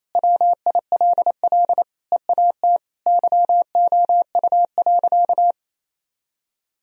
Morse code, heard as a keyed tone, about 28 wpm, 700 Hz. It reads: WILL EAT YOU.
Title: Sentences from Top 1000 Words - Review Part 1 - 28wpm